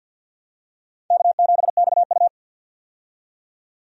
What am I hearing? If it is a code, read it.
Morse code, heard as a keyed tone, about 53 wpm, 700 Hz. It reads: K7XU
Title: USA Call Signs - 4 Characters - 50wpm